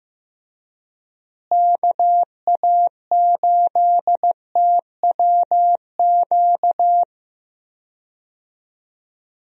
Morse code, heard as a keyed tone, about 15 wpm, 700 Hz. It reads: KA8TWQ
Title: USA Call Signs - 6 Characters - Repeated 3x - 15wpm